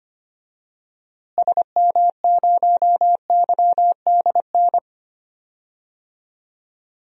Morse code, heard as a keyed tone, about 25 wpm, 700 Hz. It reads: SM0YDN
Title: DXCC-100 Call Signs - 6 Characters - Repeated 2x - 25wpm